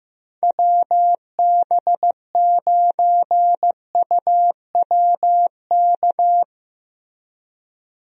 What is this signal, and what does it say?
Morse code, heard as a keyed tone, about 15 wpm, 700 Hz. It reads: WB9UWK
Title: USA Call Signs - 6 Characters - Repeated 2x - 15wpm